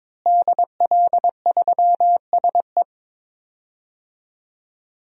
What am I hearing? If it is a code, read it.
Morse code, heard as a keyed tone, about 22 wpm, 700 Hz. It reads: DL3SE
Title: DXCC-100 Call Signs - 5 Characters - Repeated 2x - 22wpm